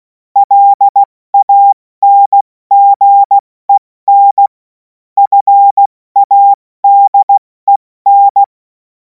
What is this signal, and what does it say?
Morse code, heard as a keyed tone, about 16 wpm, 800 Hz. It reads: LANGEN FADEN
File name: Morsekurs / Witze / Witz 17. Februar 2026 70 F80